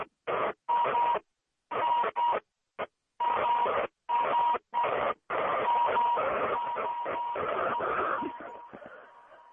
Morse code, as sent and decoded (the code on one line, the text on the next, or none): none